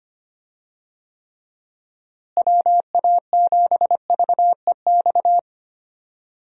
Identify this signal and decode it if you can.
Morse code, heard as a keyed tone, about 25 wpm, 700 Hz. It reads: WA7VEX
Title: USA Call Signs - 6 Characters - Repeated 2x - 25wpm